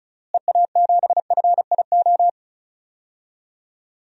Morse code, heard as a keyed tone, about 36 wpm, 700 Hz. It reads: EA7FIO
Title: DXCC-100 Call Signs - 6 Characters - Repeated 2x - 35wpm